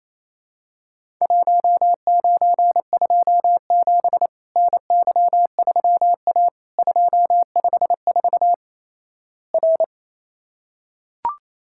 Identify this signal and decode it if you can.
Morse code, heard as a keyed tone, about 28 wpm, 700 Hz. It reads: 1927 NY3A 254 R
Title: Worked All Europe DX Contest - QTC Exchange - 28wpm